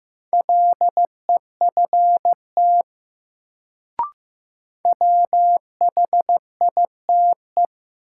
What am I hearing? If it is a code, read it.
Morse code, heard as a keyed tone, about 15 wpm, 700 Hz. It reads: LEFT WHITE
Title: Sentences from Top 300 Words - New Word Drill - 15wpm